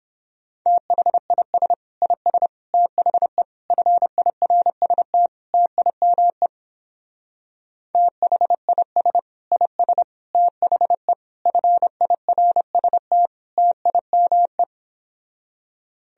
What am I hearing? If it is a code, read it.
Morse code, heard as a keyed tone, about 29 wpm, 700 Hz. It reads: THIS IS THE FIRST TIME THIS IS THE FIRST TIME